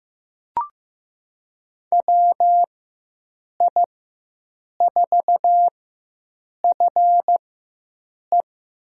Morse code, heard as a keyed tone, about 15 wpm, spaced as for 7 wpm, 700 Hz. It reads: WI4FE
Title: USA Call Signs - 4x Character Spacing - 15wpm